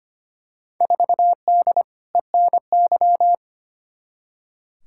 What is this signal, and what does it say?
Morse code, heard as a keyed tone, about 25 wpm, 700 Hz. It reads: 4D ENY
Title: Field Day Exchanges Encore - 25wpm